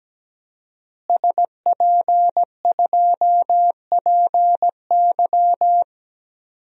Morse code, heard as a keyed tone, about 17 wpm, 700 Hz. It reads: SP2PY
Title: DXCC-100 Call Signs - 5 Characters - Repeated 2x - 17wpm